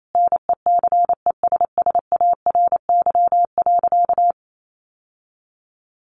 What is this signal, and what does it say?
Morse code, heard as a keyed tone, about 28 wpm, 700 Hz. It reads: NECESSARY.
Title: Sentences from Top 1500 Words 28wpm